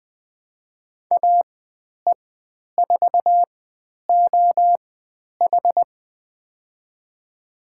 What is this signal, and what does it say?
Morse code, heard as a keyed tone, about 20 wpm, spaced as for 10 wpm, 700 Hz. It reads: AE4OH
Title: 095 Lesson 16g - Letter c - Call Signs 20wpm